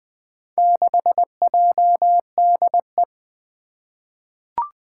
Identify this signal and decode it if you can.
Morse code, heard as a keyed tone, about 20 wpm, 700 Hz. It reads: 6JDE
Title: Four Challenging Letter-Number Combinations 20wpm